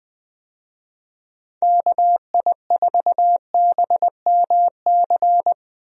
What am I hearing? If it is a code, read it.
Morse code, heard as a keyed tone, about 20 wpm, 700 Hz. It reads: KI4BMC
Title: USA Call Signs - 6 Characters - Repeated 3x - 20wpm